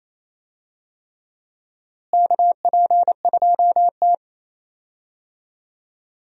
Morse code, heard as a keyed tone, about 28 wpm, 700 Hz. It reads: KP2T
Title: DXCC-100 Call Signs - 4 Characters - Repeated 3x - 28wpm